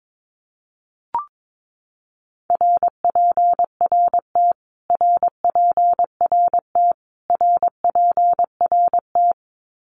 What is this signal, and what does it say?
Morse code, heard as a keyed tone, about 22 wpm, 700 Hz. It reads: RPRT RPRT RPRT
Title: Abbreviations Repeated 3x - 22wpm